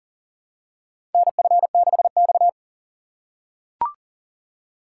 Morse code, heard as a keyed tone, about 40 wpm, 700 Hz. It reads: NF6X
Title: USA Call Signs - 4 Characters - 4x Character Spacing - 40wpm